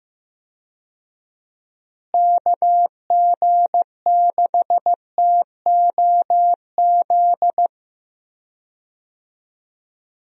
Morse code, heard as a keyed tone, about 15 wpm, 700 Hz. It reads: KG6TOZ
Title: USA Call Signs - 6 Characters - Repeated 2x - 15wpm